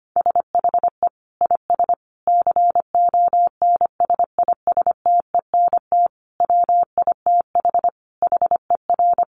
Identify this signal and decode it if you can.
Morse code, heard as a keyed tone, about 25 wpm, 700 Hz. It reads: SHE IS CONSISTENT WITH HER
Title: Sentences from Top 1500 Words 25wpm